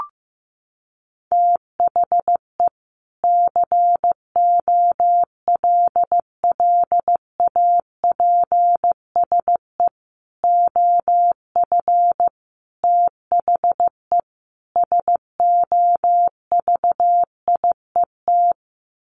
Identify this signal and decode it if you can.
Morse code, heard as a keyed tone, about 14 wpm, 700 Hz. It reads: THE COLLAPSE OF THE SOVIET
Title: Sets of 5 Words Encore 15wpm